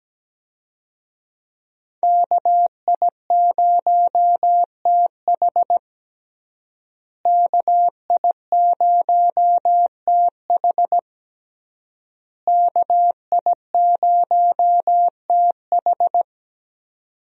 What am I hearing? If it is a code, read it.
Morse code, heard as a keyed tone, about 17 wpm, 700 Hz. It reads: KI0TH KI0TH KI0TH
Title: USA Call Signs - 5 Characters - Repeated 3x - 17wpm